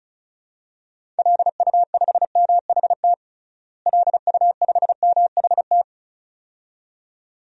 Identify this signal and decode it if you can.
Morse code, heard as a keyed tone, about 34 wpm, 700 Hz. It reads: LU5MHT LU5MHT
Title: DXCC-100 Call Signs - 6 Characters - Repeated 2x - 35wpm